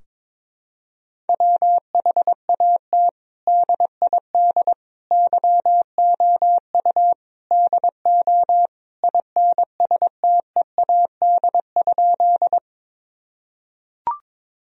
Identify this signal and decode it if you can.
Morse code, heard as a keyed tone, about 22 wpm, 700 Hz. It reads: WHAT DID YOU DO INSTEAD?
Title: Sentences from Top 1000 Words - Review Part 4 - 22wpm